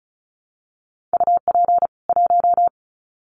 Morse code, heard as a keyed tone, about 34 wpm, 700 Hz. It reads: UP 1
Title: Common QSO Phrases 35wpm